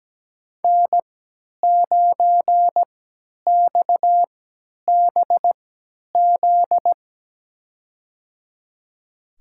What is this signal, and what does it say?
Morse code, heard as a keyed tone, about 17 wpm, spaced as for 10 wpm, 700 Hz. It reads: N9XBZ
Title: USA Call Signs - 5 Characters - 3x Character Spacing - 17wpm